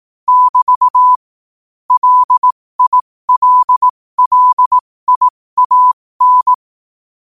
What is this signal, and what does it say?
Morse code, heard as a keyed tone, about 18 wpm, spaced as for 15 wpm, 1000 Hz. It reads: <BT> LILLIAN
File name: qotdfiles / 15wpm